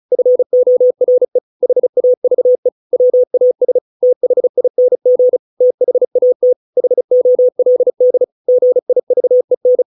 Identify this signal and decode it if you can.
Morse code, heard as a keyed tone, about 34 wpm, 500 Hz. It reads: FORE HAVE WAS THING THAT HOLD GIVEN